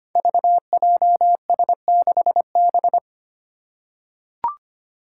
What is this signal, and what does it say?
Morse code, heard as a keyed tone, about 25 wpm, 700 Hz. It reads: VJS6B
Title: Five Challenging Letter-Number Combinations 25wpm